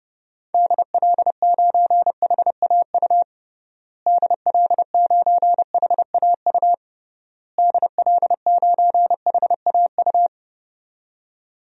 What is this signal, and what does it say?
Morse code, heard as a keyed tone, about 32 wpm, 700 Hz. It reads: DL9HAU DL9HAU DL9HAU
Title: DXCC-100 Call Signs - 6 Characters - Repeated 3x - 30wpm